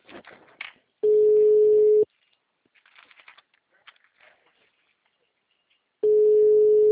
Ringback tone (Europe, 425 Hz).
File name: out- -2030- - - .77594